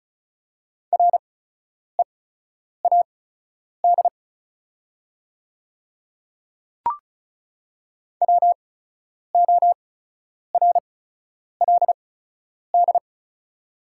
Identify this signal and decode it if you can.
Morse code, heard as a keyed tone, about 36 wpm, spaced as for 10 wpm, 700 Hz. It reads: READ WORLD
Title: 8x Character Spacing - Top 300 Words - 35wpm